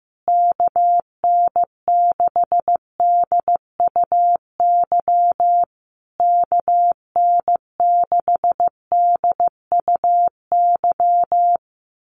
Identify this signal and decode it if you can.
Morse code, heard as a keyed tone, about 14 wpm, 700 Hz. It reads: KN6DUY KN6DUY
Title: USA Call Signs Repeated 3x 15wpm